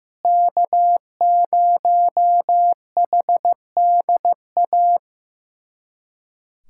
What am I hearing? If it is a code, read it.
Morse code, heard as a keyed tone, about 15 wpm, 700 Hz. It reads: K0HDA
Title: USA Callsign DE Callsign - 5 Characters - 15wpm